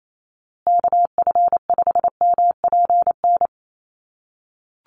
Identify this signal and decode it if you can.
Morse code, heard as a keyed tone, about 28 wpm, 700 Hz. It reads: KF5MPN
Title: USA Call Signs Repeated 2x 28wpm